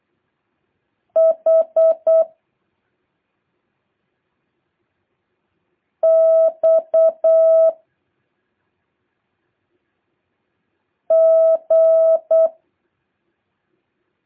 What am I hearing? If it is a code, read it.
Morse code, heard as a keyed tone, about 8 wpm, 660 Hz. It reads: H X G
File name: jh8cbh / 2023cw